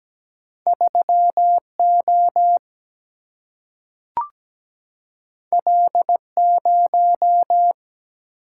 Morse code, heard as a keyed tone, about 17 wpm, 700 Hz. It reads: 3O L0
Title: Single-Letter Single-Number 17wpm